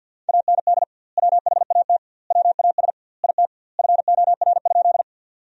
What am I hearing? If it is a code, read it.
Morse code, heard as a keyed tone, about 50 wpm, 700 Hz. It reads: AND WHAT WAS IT FOR?
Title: 2x Word-Spacing - Sentences from Top 100 Words - 50wpm